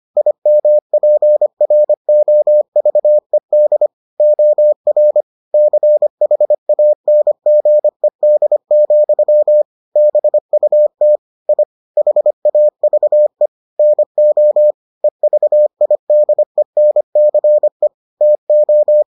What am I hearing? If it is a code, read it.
Morse code, heard as a keyed tone, about 25 wpm, 600 Hz. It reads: IMPROVED OR CHANGED, BUT I HAVE NO EVIDENCE TO